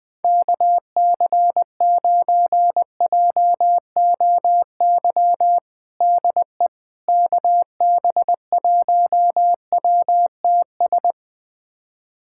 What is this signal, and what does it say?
Morse code, heard as a keyed tone, about 20 wpm, 700 Hz. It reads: KC9JOY DE KB1WTS
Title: USA Callsign DE Callsign - 6 Characters - 20wpm